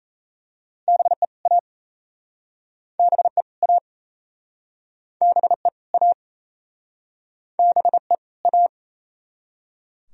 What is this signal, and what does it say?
Morse code, heard as a keyed tone, about 38 wpm, 700 Hz. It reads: BE A BE A BE A BE A